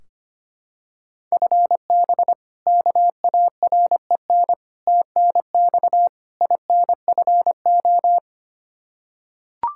Morse code, heard as a keyed tone, about 25 wpm, 700 Hz. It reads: FB KAREN TNX INFO
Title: K1USN SST QSO Practice 25wpm